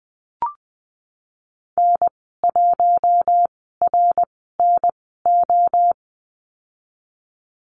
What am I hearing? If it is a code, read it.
Morse code, heard as a keyed tone, about 20 wpm, spaced as for 15 wpm, 700 Hz. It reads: N1RNO
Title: 055 Lesson 10g - Letter r - Call Signs 20wpm